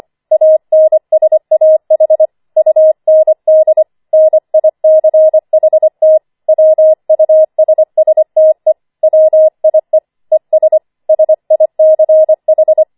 Morse code, heard as a keyed tone, about 25 wpm, 630 Hz. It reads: ANSAH UND NICHT WUSSTE WIE ES SICH
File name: 25wpm